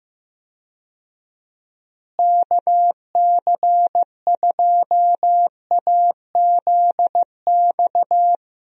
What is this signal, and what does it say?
Morse code, heard as a keyed tone, about 16 wpm, 700 Hz. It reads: KC2AZX
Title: USA Call Signs - 6 Characters - Repeated 3x - 15wpm